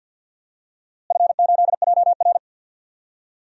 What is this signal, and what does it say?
Morse code, heard as a keyed tone, about 49 wpm, 700 Hz. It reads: F8JR
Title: DXCC-100 Call Signs - 4 Characters - Repeated 3x - 50wpm